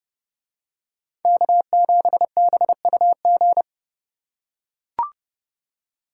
Morse code, heard as a keyed tone, about 29 wpm, 700 Hz. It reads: K7BUG